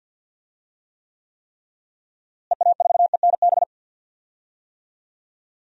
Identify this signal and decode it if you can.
Morse code, heard as a keyed tone, about 52 wpm, 700 Hz. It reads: EA4ENB
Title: DXCC-100 Call Signs - 6 Characters - Repeated 3x - 50wpm